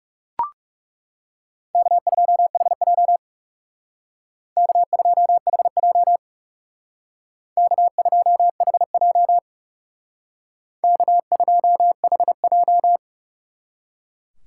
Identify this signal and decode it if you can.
Morse code, heard as a keyed tone, about 38 wpm, 700 Hz. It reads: K2HJ K2HJ K2HJ K2HJ